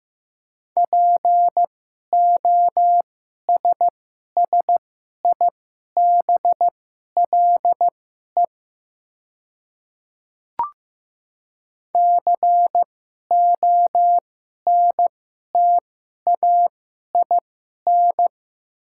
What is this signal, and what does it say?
Morse code, heard as a keyed tone, about 15 wpm, spaced as for 11 wpm, 700 Hz. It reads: POSSIBLE CONTAIN
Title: 2x Character Spacing - Top 500 Words - 15wpm